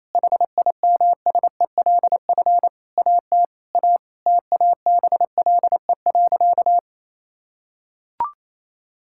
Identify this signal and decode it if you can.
Morse code, heard as a keyed tone, about 28 wpm, 700 Hz. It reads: HIMSELF AT A TABLE.
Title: Sentences from Top 1600 Words 28wpm